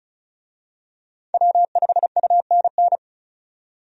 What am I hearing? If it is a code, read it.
Morse code, heard as a keyed tone, about 36 wpm, 700 Hz. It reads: W5UNN
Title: USA Call Signs - 5 Characters - 35wpm